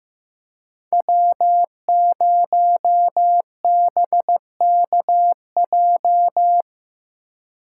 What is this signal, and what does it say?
Morse code, heard as a keyed tone, about 18 wpm, 700 Hz. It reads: W0BKJ
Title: Speed Racing - USA 1x3 Call Signs 15wpm